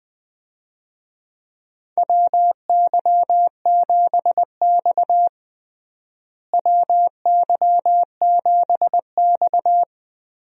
Morse code, heard as a keyed tone, about 20 wpm, 700 Hz. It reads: WY7X WY7X
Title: USA Call Signs - 4 Characters - Repeated 3x - 20wpm